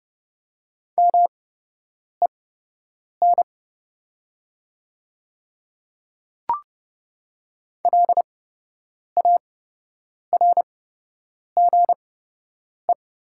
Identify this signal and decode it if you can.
Morse code, heard as a keyed tone, about 30 wpm, spaced as for 8 wpm, 700 Hz. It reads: MEN LARGE
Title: 8x Character Spacing - Top 200 Words - 30wpm